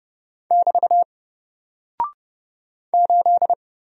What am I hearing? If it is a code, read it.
Morse code, heard as a keyed tone, about 30 wpm, 700 Hz. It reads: <BT> 8
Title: Single Character (Letter:Number:Punctuation:Prosign) 30wpm